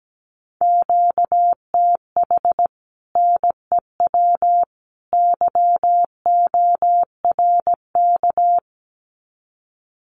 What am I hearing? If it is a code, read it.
Morse code, heard as a keyed tone, about 17 wpm, 700 Hz. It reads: QTH NEW YORK
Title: Common QSO Phrases 17wpm